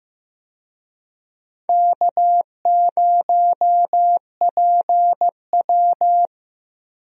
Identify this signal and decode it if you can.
Morse code, heard as a keyed tone, about 15 wpm, 700 Hz. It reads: K0PW
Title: USA Call Signs - 4 Characters - Repeated 3x - 15wpm